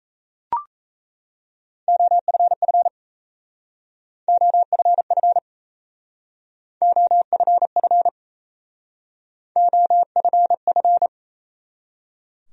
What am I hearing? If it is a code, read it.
Morse code, heard as a keyed tone, about 36 wpm, 700 Hz. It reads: OFF OFF OFF OFF